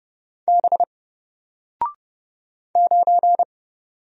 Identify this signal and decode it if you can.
Morse code, heard as a keyed tone, about 31 wpm, 700 Hz. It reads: B 9
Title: Single Letter-Number 30wpm